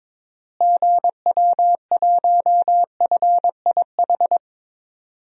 Morse code, heard as a keyed tone, about 22 wpm, 700 Hz. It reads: GW1FIH
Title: DXCC-100 Call Signs - 6 Characters - Repeated 3x - 22wpm